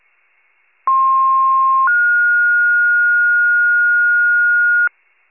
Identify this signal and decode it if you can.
Two-tone page, heard as a one-second tone, then a three-second tone, about 1060 then 1430 Hz, with noise. It is heard from 0.9 to 4.9 s.
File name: pager / 98 RESCUE / wcalert